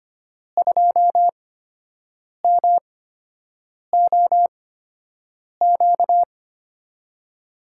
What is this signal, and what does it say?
Morse code, heard as a keyed tone, about 25 wpm, spaced as for 7 wpm, 700 Hz. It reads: 2MOQ
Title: USA Call Signs - 5 Characters - 8x Character Spacing - 25wpm